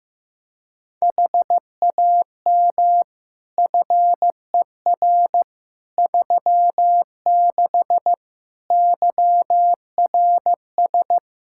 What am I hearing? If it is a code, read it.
Morse code, heard as a keyed tone, about 16 wpm, 700 Hz. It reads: HAM FER 36 YRS
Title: K1USN SST QSO Practice 15wpm